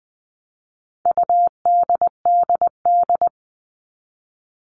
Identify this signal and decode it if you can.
Morse code, heard as a keyed tone, about 20 wpm, 700 Hz. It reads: UDDD
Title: All Cut Numbers - 4-Digit 20wpm